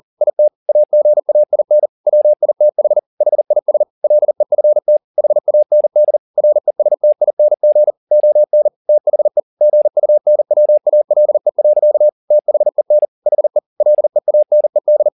Morse code, heard as a keyed tone, about 40 wpm, 600 Hz. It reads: IT AGAIN WITH HIS LEFT HAND RESTING ON THE GUNWALE. THEN HE LEANED
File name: the old man and the sea 2 40wpm